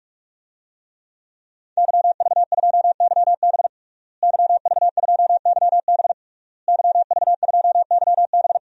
Morse code, heard as a keyed tone, about 46 wpm, 700 Hz. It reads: YV2YB YV2YB YV2YB
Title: DXCC-100 Call Signs - 5 Characters - Repeated 3x - 45wpm